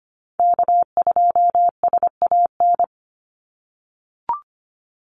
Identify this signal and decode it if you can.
Morse code, heard as a keyed tone, about 25 wpm, 700 Hz. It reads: K2SAN
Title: Call Signs - 25wpm